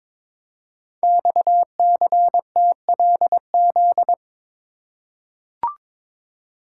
Morse code, heard as a keyed tone, about 23 wpm, 700 Hz. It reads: XCTLZ
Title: Code Groups - 5 Letters - 22wpm